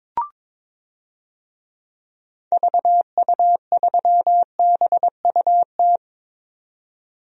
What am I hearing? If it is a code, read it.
Morse code, heard as a keyed tone, about 22 wpm, 700 Hz. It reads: VU3BUT
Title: DXCC-100 Call Signs - 6 Characters - 22wpm